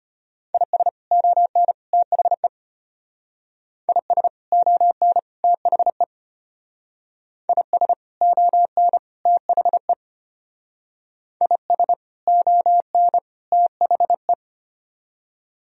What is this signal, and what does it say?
Morse code, heard as a keyed tone, about 31 wpm, 700 Hz. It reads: IS ON THE IS ON THE IS ON THE IS ON THE
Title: Speed-Racing - Sets of 3 Words 25wpm